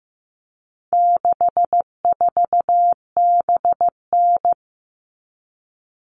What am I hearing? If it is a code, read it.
Morse code, heard as a keyed tone, about 15 wpm, 700 Hz. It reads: 64BN
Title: All Cut Numbers - 4-Digit 15wpm